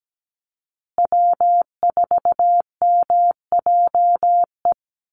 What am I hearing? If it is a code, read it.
Morse code, heard as a keyed tone, about 17 wpm, 700 Hz. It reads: W4MJE
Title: Call Signs Repeated 2x - 17wpm